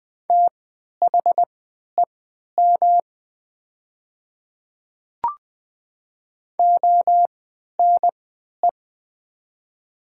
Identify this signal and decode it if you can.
Morse code, heard as a keyed tone, about 20 wpm, spaced as for 11 wpm, 700 Hz. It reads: THEM ONE
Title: 3x Character Spacing - Top 100 Words - 20wpm